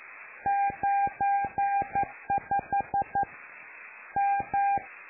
Morse code, heard as a keyed tone, about 11 wpm, 800 Hz. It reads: 95 M
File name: lu4aao / Globo Amsat